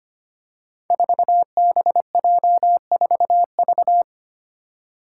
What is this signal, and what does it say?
Morse code, heard as a keyed tone, about 25 wpm, 700 Hz. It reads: 4BJ4V
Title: Five Challenging Letter-Number Combinations 25wpm